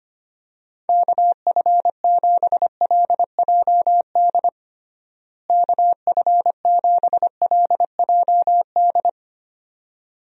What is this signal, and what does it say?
Morse code, heard as a keyed tone, about 25 wpm, 700 Hz. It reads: KF7LJD KF7LJD
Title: USA Call Signs - 6 Characters - Repeated 2x - 25wpm